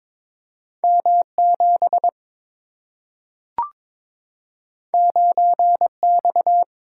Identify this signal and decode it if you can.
Morse code, heard as a keyed tone, about 23 wpm, 700 Hz. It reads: M7 9X
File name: practice / 22wpm / Two Letter-Number Combinations - 22wpm